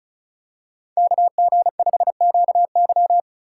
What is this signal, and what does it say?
Morse code, heard as a keyed tone, about 36 wpm, 700 Hz. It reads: KG5QY
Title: USA Call Signs - 5 Characters - 2x Character Spacing - 35wpm